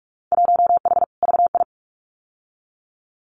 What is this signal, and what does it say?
Morse code, heard as a keyed tone, about 46 wpm, 700 Hz. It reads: 1H VI
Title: Winter Field Day Exchanges Encore 45wpm